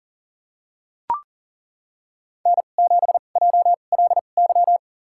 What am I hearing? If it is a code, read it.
Morse code, heard as a keyed tone, about 38 wpm, spaced as for 28 wpm, 700 Hz. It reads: N7JLY
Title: USA Call Signs - 2x Character Spacing - 40wpm